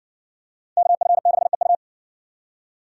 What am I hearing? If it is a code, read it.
Morse code, heard as a keyed tone, about 60 wpm, 700 Hz. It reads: KF6EU